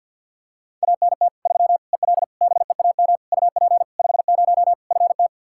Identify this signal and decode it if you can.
Morse code, heard as a keyed tone, about 52 wpm, 700 Hz. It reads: ANT 3 EL BEAM UP 50 FT
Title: Practice QSO - Exchange 2 - Info - 50wpm